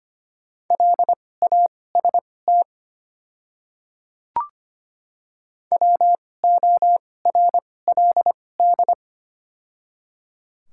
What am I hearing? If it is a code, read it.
Morse code, heard as a keyed tone, about 25 wpm, spaced as for 18 wpm, 700 Hz. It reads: LAST WORLD
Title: 2x Character Spacing - Top 300 Words - 25wpm